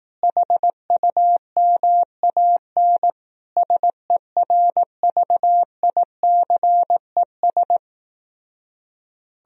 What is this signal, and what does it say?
Morse code, heard as a keyed tone, about 18 wpm, 700 Hz. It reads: HUMAN SERVICES
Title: Speed-Racing - Sets of 5 Words 15wpm